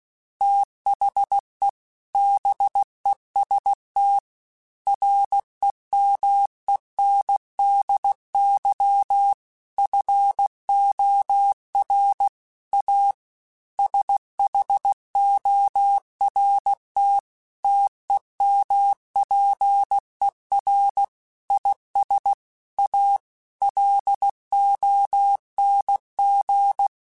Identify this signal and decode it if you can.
Morse code, heard as a keyed tone, about 16 wpm, 800 Hz. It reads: THE BEST REMENDY FOR A SHORT TEMPER IS A LONG
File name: Quotes / Quotes-mp3s / Temper @15